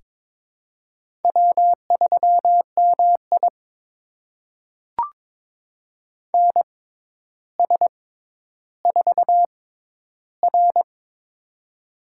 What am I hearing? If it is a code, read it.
Morse code, heard as a keyed tone, about 22 wpm, spaced as for 11 wpm, 700 Hz. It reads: W3MI NS4R
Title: USA Call Signs - 4 Characters - 6x Character Spacing - 22wpm